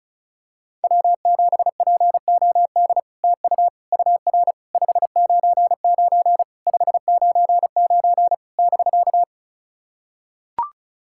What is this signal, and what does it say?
Morse code, heard as a keyed tone, about 34 wpm, 700 Hz. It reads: W7POD TU UR 599 599 <BK>
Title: SOTA QSO Practice 35wpm